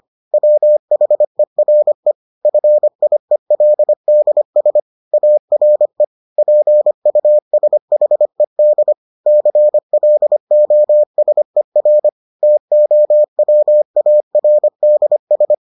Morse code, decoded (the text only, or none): WHERE FIELDS ARE PUSHED CLOSER TOWARDS